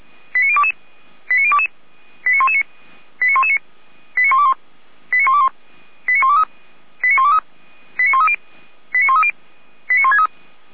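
Five-tone selective call (ZVEI), heard as quick runs of five tones, tones of about 70 ms. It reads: 89022 89022 89119 89119 89121 89121 89123 89123 89130 89130 89163
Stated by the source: Five-tone ZVEI selective call received on an SDR.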